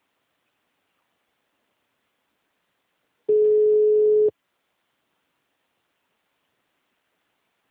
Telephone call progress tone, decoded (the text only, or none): ringback tone